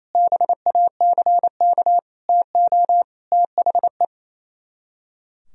Morse code, heard as a keyed tone, about 26 wpm, 700 Hz. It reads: BACK TO THE
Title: Speed-Racing - Sets of 4 Words 28wpm